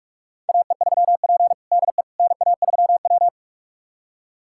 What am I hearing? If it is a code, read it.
Morse code, heard as a keyed tone, about 47 wpm, 700 Hz. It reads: AE3P DE NA3W
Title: USA Callsign DE Callsign - 4 Characters - 45wpm